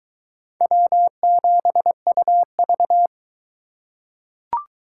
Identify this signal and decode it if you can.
Morse code, heard as a keyed tone, about 21 wpm, 700 Hz. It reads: W7UV